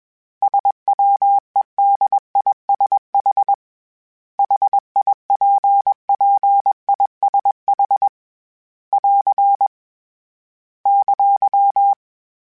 Morse code, heard as a keyed tone, about 21 wpm, 800 Hz. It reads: SWEDISH HIPPISH <AR> !